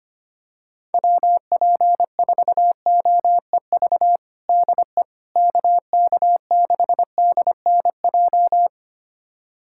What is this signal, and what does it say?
Morse code, heard as a keyed tone, about 25 wpm, 700 Hz. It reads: WP4OEV DE KK6DNJ
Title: USA Callsign DE Callsign - 6 Characters - 25wpm